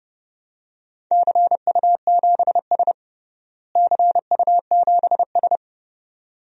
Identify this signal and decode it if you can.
Morse code, heard as a keyed tone, about 31 wpm, 700 Hz. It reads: CU7S CU7S
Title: DXCC-100 Call Signs - 4 Characters - Repeated 2x - 30wpm